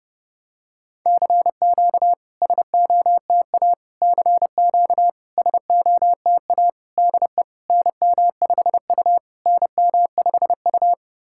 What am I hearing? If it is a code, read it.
Morse code, heard as a keyed tone, about 28 wpm, 700 Hz. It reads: CQ SOTA CQ SOTA DE NM5U NM5U
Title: SOTA QSO Practice 30wpm